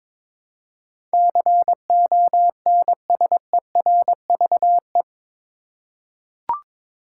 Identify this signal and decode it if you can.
Morse code, heard as a keyed tone, about 22 wpm, 700 Hz. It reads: CONSERVE
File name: practice / 22wpm / Eight Letter Words - 22wpm